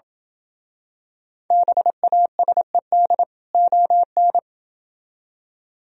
Morse code, heard as a keyed tone, about 27 wpm, 700 Hz. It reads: BASED ON